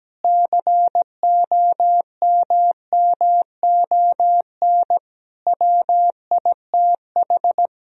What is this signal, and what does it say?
Morse code, heard as a keyed tone, about 17 wpm, 700 Hz. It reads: COMMON WITH
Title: Sentences from Top 1000 Words - Review Part 4 - 17wpm